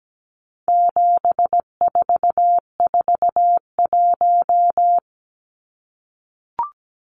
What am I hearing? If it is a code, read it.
Morse code, heard as a keyed tone, about 17 wpm, 700 Hz. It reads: 7441
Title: Numbers - 4-Digit 17wpm